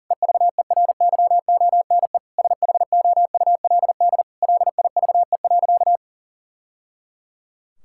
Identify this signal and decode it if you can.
Morse code, heard as a keyed tone, about 40 wpm, 700 Hz. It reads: EVERYONE SHOULD LIVE.
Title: Sentences from Top 1000 Words - Review Part 5 - 40wpm